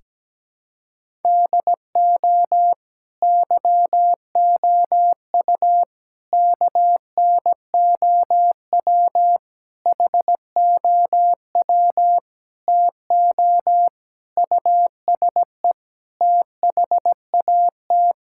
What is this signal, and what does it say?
Morse code, heard as a keyed tone, about 17 wpm, 700 Hz. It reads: DO YOU KNOW HOW TO USE THAT
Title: Sentences from Top 1000 Words - Review Part 6 - 17wpm